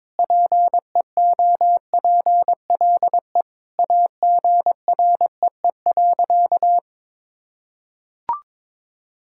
Morse code, heard as a keyed tone, about 22 wpm, 700 Hz. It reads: PEOPLE AGREE.
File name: practice / 22wpm / Sentences with 4-Word Phrases - 22wpm